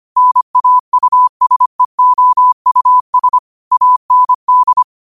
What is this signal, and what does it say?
Morse code, heard as a keyed tone, about 25 wpm, 1000 Hz. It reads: NAUSEOUS AND
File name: qotdfiles / 25wpm